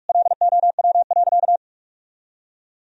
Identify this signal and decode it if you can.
Morse code, heard as a keyed tone, about 44 wpm, 700 Hz. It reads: LOW.
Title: Sentences from Top 1000 Words - Review Part 1 - 45wpm